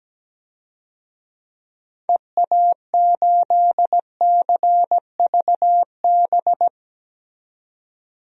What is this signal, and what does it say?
Morse code, heard as a keyed tone, about 17 wpm, 700 Hz. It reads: EA8CVB
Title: DXCC-100 Call Signs - 6 Characters - 17wpm